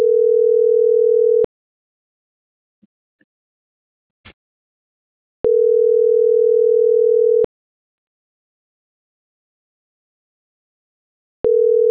Ringback tone (North America), 440 + 480 Hz.